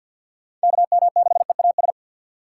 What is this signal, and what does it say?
Morse code, heard as a keyed tone, about 52 wpm, 700 Hz. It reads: KM6EAS
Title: USA Call Signs - 3x Character Spacing - 50wpm